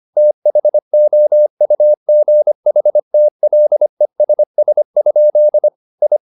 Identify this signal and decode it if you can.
Morse code, heard as a keyed tone, about 25 wpm, 600 Hz. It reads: THOUGHTLESS? I
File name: the old man and the sea part 1 25wpm